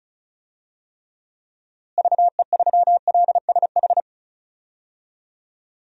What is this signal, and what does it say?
Morse code, heard as a keyed tone, about 34 wpm, 700 Hz. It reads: VE3LSH
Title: DXCC-100 Call Signs - 6 Characters - Repeated 3x - 35wpm